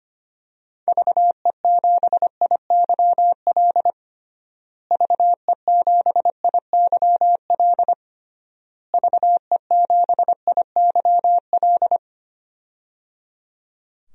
Morse code, heard as a keyed tone, about 25 wpm, 700 Hz. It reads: VE7IYL VE7IYL VE7IYL
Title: DXCC-100 Call Signs - 6 Characters - Repeated 3x - 25wpm